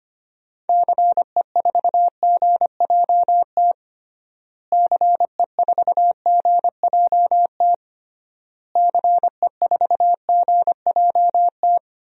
Morse code, heard as a keyed tone, about 25 wpm, 700 Hz. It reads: CE4GJT CE4GJT CE4GJT
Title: DXCC-100 Call Signs - 6 Characters - Repeated 3x - 25wpm